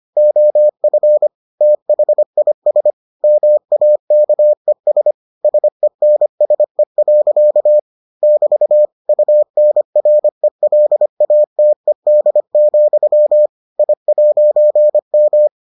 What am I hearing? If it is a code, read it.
Morse code, heard as a keyed tone, about 25 wpm, 600 Hz. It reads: OF THIS MAKES SENSE. <BT> UNRELATED, I'M